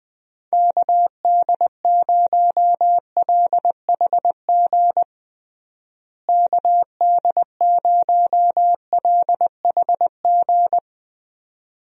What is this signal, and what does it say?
Morse code, heard as a keyed tone, about 20 wpm, 700 Hz. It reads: KD0LHG KD0LHG
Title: USA Call Signs - 6 Characters - Repeated 2x - 20wpm